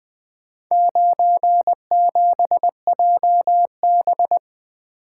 Morse code, heard as a keyed tone, about 20 wpm, 700 Hz. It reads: 97JB
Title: Four Letter-Number Combinations - 20wpm